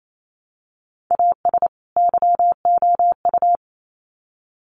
Morse code, heard as a keyed tone, about 28 wpm, 700 Hz. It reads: AS YOU
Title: Sets of 2 Words 28wpm